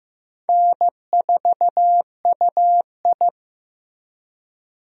Morse code, heard as a keyed tone, about 15 wpm, 700 Hz. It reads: N4UI
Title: USA Call Signs - 4 Characters - Repeated 2x - 15wpm